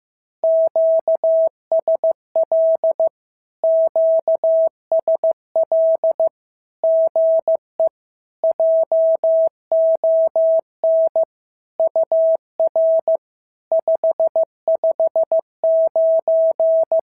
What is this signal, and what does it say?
Morse code, heard as a keyed tone, about 15 wpm, 650 Hz. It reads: QSL QSL GE JON UR 559
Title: SOTA QSO Practice 15wpm